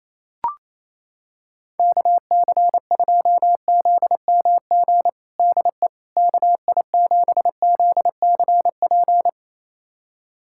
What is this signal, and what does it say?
Morse code, heard as a keyed tone, about 28 wpm, 700 Hz. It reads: KC2ZMG DE KI7ZCP
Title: USA Callsign DE Callsign - 6 Characters - 28wpm